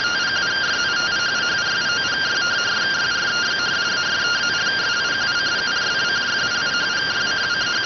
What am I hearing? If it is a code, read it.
Two-tone teleprinter signal, about 170 Hz shift.